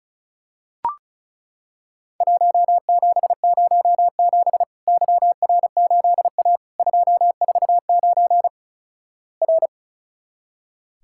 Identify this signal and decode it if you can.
Morse code, heard as a keyed tone, about 36 wpm, 700 Hz. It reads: 1707 YR8A 249 R
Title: Worked All Europe DX Contest - QTC Exchange - 35wpm